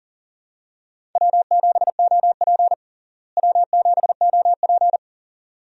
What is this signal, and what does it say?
Morse code, heard as a keyed tone, about 37 wpm, 700 Hz. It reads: W7OP W7OP
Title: USA Call Signs - 4 Characters - Repeated 2x - 40wpm